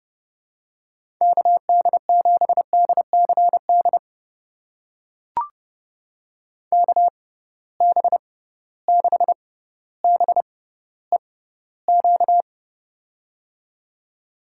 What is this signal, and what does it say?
Morse code, heard as a keyed tone, about 30 wpm, spaced as for 15 wpm, 700 Hz. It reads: KD7DCD KB6BEQ
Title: USA Call Signs - 6 Characters - 6x Character Spacing - 30wpm